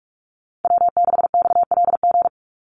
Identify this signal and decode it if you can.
Morse code, heard as a keyed tone, about 44 wpm, 700 Hz. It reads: W6XLG
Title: USA Call Signs 45wpm